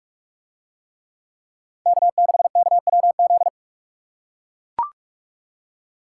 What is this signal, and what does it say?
Morse code, heard as a keyed tone, about 41 wpm, 700 Hz. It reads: K6KWZ